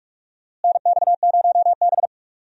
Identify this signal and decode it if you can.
Morse code, heard as a keyed tone, about 46 wpm, 700 Hz. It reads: NX0B
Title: USA Call Signs - 4 Characters - Repeated 3x - 45wpm